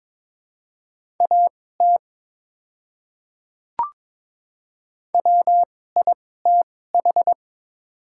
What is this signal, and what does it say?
Morse code, heard as a keyed tone, about 23 wpm, spaced as for 16 wpm, 700 Hz. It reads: AT WITH
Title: 2x Character Spacing - Top 100 Words - 22wpm